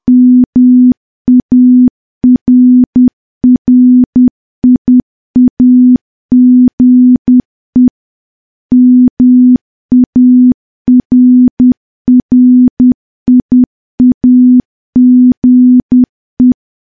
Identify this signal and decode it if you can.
Morse code, heard as a keyed tone, about 10 wpm, 250 Hz. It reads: MARRIAGE MARRIAGE
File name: morse / morse-training-10WPM